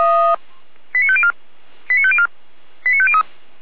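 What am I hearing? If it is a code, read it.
Five-tone selective call (ZVEI), heard as quick runs of five tones, tones of about 70 ms. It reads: 89504 89504 89503
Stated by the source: ZVEI five-tone call received by SDR, recorded by fire brigade Feuerwehr Orlamünde.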